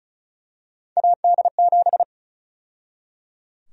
Morse code, heard as a keyed tone, about 36 wpm, 700 Hz. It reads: AD7
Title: USA Call Sign Prefixes - 35wpm